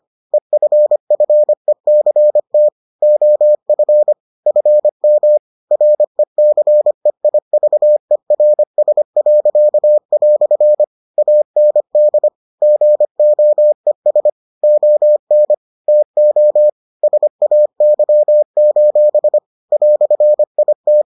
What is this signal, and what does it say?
Morse code, heard as a keyed tone, about 25 wpm, 600 Hz. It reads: EFFECT OF FM RECEIVERS." AND GOES ON TO SAY: "IT